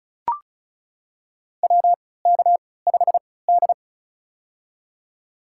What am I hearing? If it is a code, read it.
Morse code, heard as a keyed tone, about 36 wpm, spaced as for 20 wpm, 700 Hz. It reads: WK5D
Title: USA Call Signs - 4 Characters - 3x Character Spacing - 35wpm